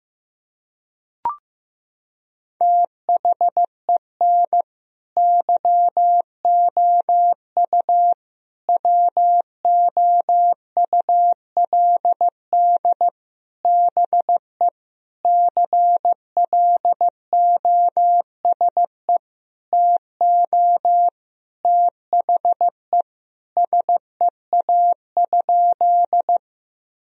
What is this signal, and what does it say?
Morse code, heard as a keyed tone, about 16 wpm, 700 Hz. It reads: THEN YOU WOULD BE CLOSE TO THE SEA?
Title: Sentences from Top 1000 Words - Review Part 1 - 15wpm